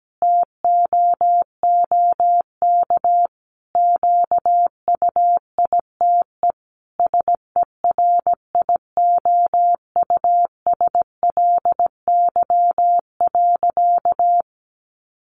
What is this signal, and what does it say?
Morse code, heard as a keyed tone, about 17 wpm, 700 Hz. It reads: TOOK QUITE SERIOUSLY.
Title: Sentences from Top 1200 Words 17wpm